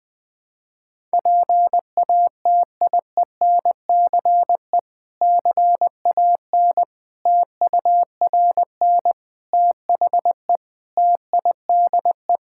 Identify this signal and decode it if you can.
Morse code, heard as a keyed tone, about 20 wpm, 700 Hz. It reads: PATIENCE CAN TURN THE TIDE
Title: Difficult Sentences with Hints - 20wpm